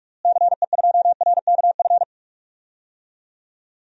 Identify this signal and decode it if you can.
Morse code, heard as a keyed tone, about 43 wpm, 700 Hz. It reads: CE2RKF
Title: DXCC-100 Call Signs - 6 Characters - Repeated 2x - 45wpm